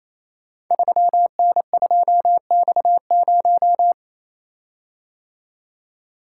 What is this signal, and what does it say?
Morse code, heard as a keyed tone, about 28 wpm, 700 Hz. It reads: 3N2X0
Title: Code Groups - 5 Letter:Numbers - 28wpm